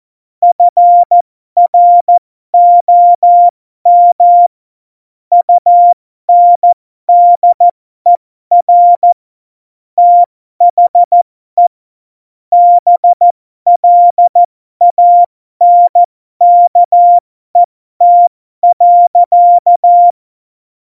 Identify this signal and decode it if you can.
Morse code, heard as a keyed tone, about 14 wpm, spaced as for 12 wpm, 700 Hz. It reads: FROM UNDER THE BLANKET.
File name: Morsekurs / Witze